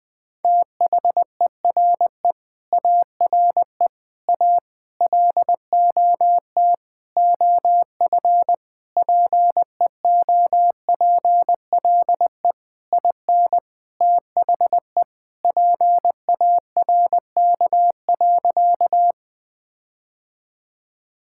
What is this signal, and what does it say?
Morse code, heard as a keyed tone, about 20 wpm, 700 Hz. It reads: THERE ARE A LOT OF PEOPLE IN THE PARK.
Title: Sentences with 5-Word Phrases - Spoken Beginning - 20wpm